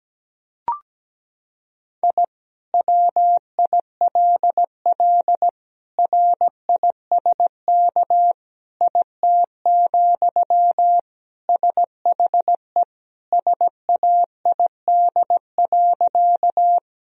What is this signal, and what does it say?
Morse code, heard as a keyed tone, about 17 wpm, 700 Hz. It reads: I WILL RISK IT, SHE SAID.
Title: Sentences from Top 1000 Words - Review Part 4 - 17wpm